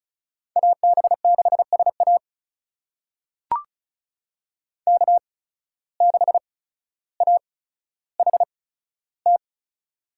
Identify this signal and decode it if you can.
Morse code, heard as a keyed tone, about 34 wpm, spaced as for 15 wpm, 700 Hz. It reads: AB6SA K6AHT
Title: USA Call Signs - 5 Characters - 8x Character Spacing - 35wpm